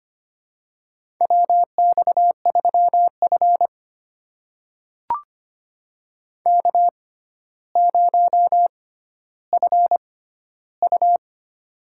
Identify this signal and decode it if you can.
Morse code, heard as a keyed tone, about 25 wpm, spaced as for 13 wpm, 700 Hz. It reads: WX3F K0FU
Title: USA Call Signs - 4 Characters - 6x Character Spacing - 25wpm